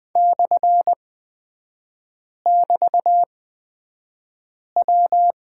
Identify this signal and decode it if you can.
Morse code, heard as a keyed tone, about 20 wpm, spaced as for 5 wpm, 700 Hz. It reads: /<BT>W
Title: 224 Lesson 36e - BT - Three Characters 20wpm